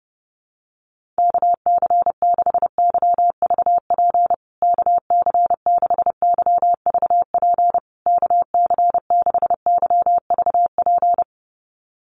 Morse code, heard as a keyed tone, about 30 wpm, 700 Hz. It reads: KC6YVP KC6YVP KC6YVP
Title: USA Call Signs Repeated 3x 30wpm